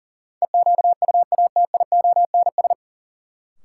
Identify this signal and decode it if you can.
Morse code, heard as a keyed tone, about 37 wpm, 700 Hz. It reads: EQUATIONS
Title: Sets of 5 Words - New Word Drill - 40wpm